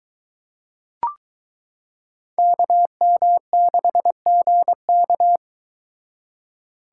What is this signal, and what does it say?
Morse code, heard as a keyed tone, about 22 wpm, 700 Hz. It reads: KM6GK